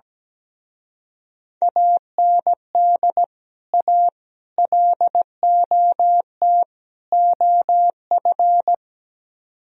Morse code, heard as a keyed tone, about 17 wpm, 700 Hz. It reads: AND A LOT OF
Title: Speed-Racing - Sets of 4 Words 17wpm